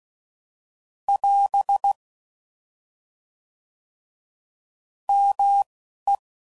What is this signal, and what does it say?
Morse code, heard as a keyed tone, about 16 wpm, spaced as for 11 wpm, 800 Hz. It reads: <AS> ME